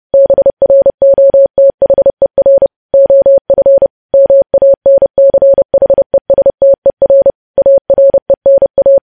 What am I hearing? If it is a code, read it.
Morse code, heard as a keyed tone, about 30 wpm, 550 Hz. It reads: BROTHER OF MANCHESTER ARENA
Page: news headlines in morse code at 30 wpm